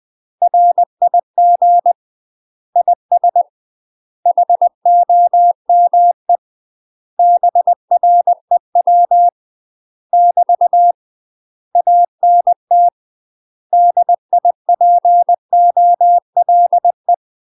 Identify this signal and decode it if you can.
Morse code, heard as a keyed tone, about 20 wpm, 700 Hz. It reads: RIG IS HOME BREW <BT> ANT DIPOLE
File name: UK MASTER QSOs TRAINING / 20WPM / UK MASTER QSOs 0008